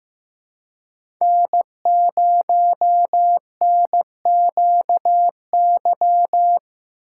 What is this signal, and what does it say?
Morse code, heard as a keyed tone, about 15 wpm, 700 Hz. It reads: N0NQY
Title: USA Call Signs - 4x Character Spacing - 15wpm